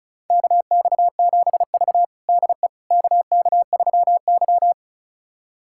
Morse code, heard as a keyed tone, about 34 wpm, 700 Hz. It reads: KX7V DE KK3Y
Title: USA Callsign DE Callsign - 4 Characters - 35wpm